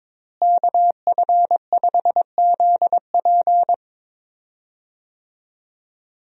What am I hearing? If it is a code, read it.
Morse code, heard as a keyed tone, about 22 wpm, 700 Hz. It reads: KF5ZP
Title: USA Call Signs - 5 Characters - Repeated 3x - 22wpm